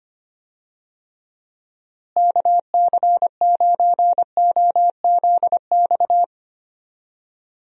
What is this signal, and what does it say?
Morse code, heard as a keyed tone, about 25 wpm, 700 Hz. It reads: KC9OZX
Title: USA Call Signs - 6 Characters - Repeated 2x - 25wpm